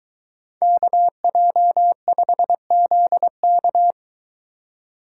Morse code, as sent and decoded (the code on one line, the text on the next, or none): -.- .--- ..... --.. -.-
KJ5ZK